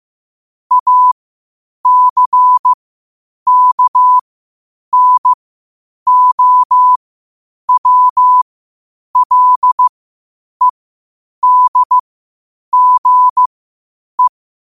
Morse code, heard as a keyed tone, about 15 wpm, spaced as for 8 wpm, 1000 Hz. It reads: ACKNOWLEDGE
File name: qotdfiles / 8wpm